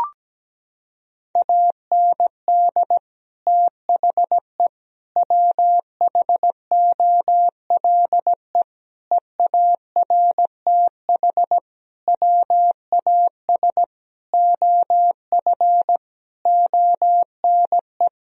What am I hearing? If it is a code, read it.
Morse code, heard as a keyed tone, about 17 wpm, 700 Hz. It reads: AND THE WHOLE EARTH WAS OF ONE
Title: Sentences from Top 1000 Words - Review Part 3 - 17wpm